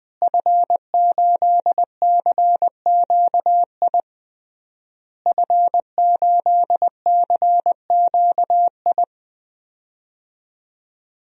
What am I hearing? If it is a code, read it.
Morse code, heard as a keyed tone, about 20 wpm, 700 Hz. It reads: F8CQI F8CQI
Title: DXCC-100 Call Signs - 5 Characters - Repeated 3x - 20wpm